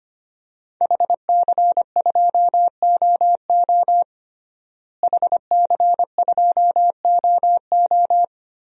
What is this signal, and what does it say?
Morse code, heard as a keyed tone, about 25 wpm, 700 Hz. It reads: HC2OO HC2OO
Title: DXCC-100 Call Signs - 5 Characters - Repeated 2x - 25wpm